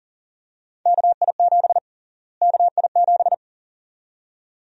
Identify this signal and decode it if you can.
Morse code, heard as a keyed tone, about 40 wpm, 700 Hz. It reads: KI7 KI7
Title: USA Call Sign Prefixes - Repeated 2x - 40wpm